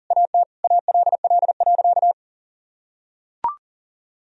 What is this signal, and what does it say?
Morse code, heard as a keyed tone, about 40 wpm, 700 Hz. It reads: AT ALL.
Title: Sentences from Top 1000 Words - Review Part 2 - 40wpm